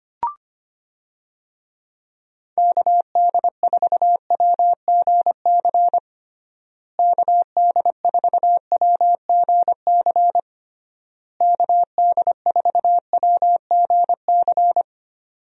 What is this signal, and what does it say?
Morse code, heard as a keyed tone, about 25 wpm, 700 Hz. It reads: KD4WGC KD4WGC KD4WGC
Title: USA Call Signs - 6 Characters - Repeated 3x - 25wpm